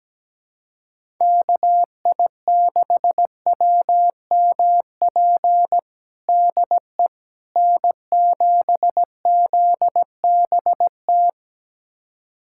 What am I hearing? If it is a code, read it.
Morse code, heard as a keyed tone, about 17 wpm, 700 Hz. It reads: KI6WMP DE N7ZBT
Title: USA Callsign DE Callsign 17wpm